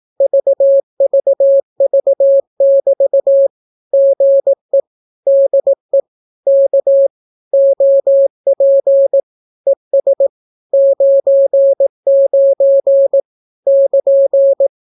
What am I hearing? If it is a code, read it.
Morse code, heard as a keyed tone, about 18 wpm, 550 Hz. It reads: VVV<BT> GE DE K OP ES 99 <KN>
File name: naukapodstaw / skroty / skroty